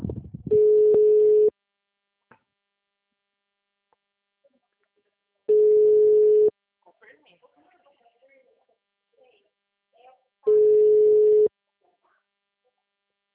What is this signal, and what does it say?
Ringback tone (Europe, 425 Hz), with speech.